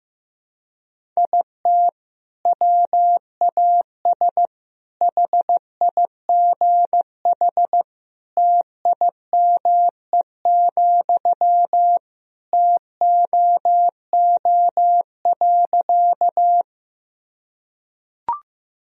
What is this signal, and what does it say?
Morse code, heard as a keyed tone, about 15 wpm, 700 Hz. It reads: IT WAS HIGH TIME, TOO.
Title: Sentences from Top 1000 Words - Review Part 1 - 15wpm